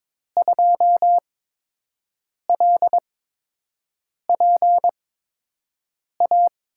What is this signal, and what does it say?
Morse code, heard as a keyed tone, about 22 wpm, spaced as for 6 wpm, 700 Hz. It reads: 2LPA
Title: USA Call Signs - 6 Characters - 8x Character Spacing - 22wpm